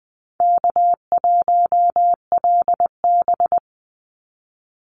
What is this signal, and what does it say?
Morse code, heard as a keyed tone, about 20 wpm, 690 Hz. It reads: K1LB
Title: NA SOTA Chaser Callsigns 20wpm